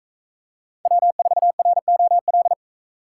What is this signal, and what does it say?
Morse code, heard as a keyed tone, about 41 wpm, 700 Hz. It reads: W4ROL